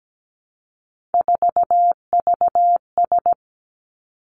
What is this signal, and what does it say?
Morse code, heard as a keyed tone, about 17 wpm, 700 Hz. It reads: 4VS
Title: Three Challenging Letter-Number Combinations 17wpm